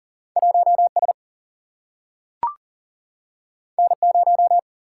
Morse code, heard as a keyed tone, about 40 wpm, 700 Hz. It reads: 1 S N 0
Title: Single-Letter Single-Number 40wpm